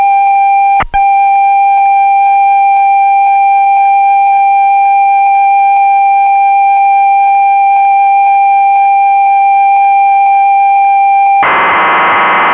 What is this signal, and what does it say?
Computer data on cassette.